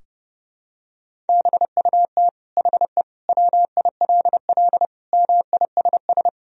Morse code, heard as a keyed tone, about 30 wpm, 700 Hz. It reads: BUT HE WILL MISS
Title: Sentences from Top 1000 Words - Review Part 3 - 30wpm